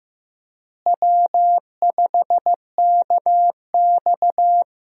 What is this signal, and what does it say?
Morse code, heard as a keyed tone, about 15 wpm, 700 Hz. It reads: W5KX
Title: USA Call Signs - 4 Characters - Repeated 3x - 15wpm